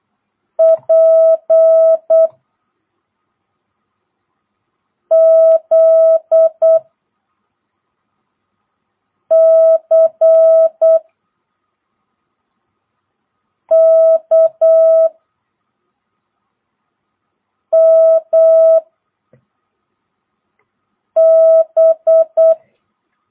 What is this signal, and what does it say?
Morse code, heard as a keyed tone, about 8 wpm, 1300 Hz. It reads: P Z C K M B
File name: jh8cbh / 2022cw